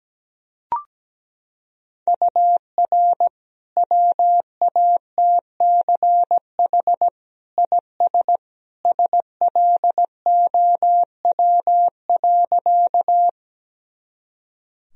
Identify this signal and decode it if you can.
Morse code, heard as a keyed tone, about 17 wpm, 700 Hz. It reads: UR WATCH IS SLOW.
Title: Sentences from Top 400 Words and Abbreviations 17wpm